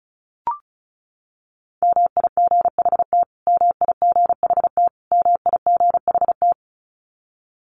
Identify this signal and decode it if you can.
Morse code, heard as a keyed tone, about 34 wpm, 700 Hz. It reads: MIGHT MIGHT MIGHT
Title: Top 300 Words Repeated 35wpm